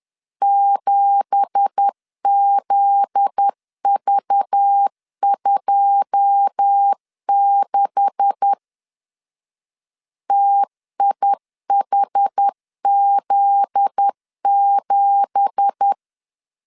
Morse code, heard as a keyed tone, about 10 wpm, 800 Hz. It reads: 7ZV26 TIHZ7
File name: LEZIONI / 9 2867 10wpm 2733